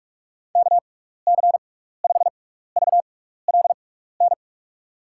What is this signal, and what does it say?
Morse code, heard as a keyed tone, about 46 wpm, spaced as for 15 wpm, 700 Hz. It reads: KC5VLN
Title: USA Call Signs - 6 Characters - 6x Character Spacing - 45wpm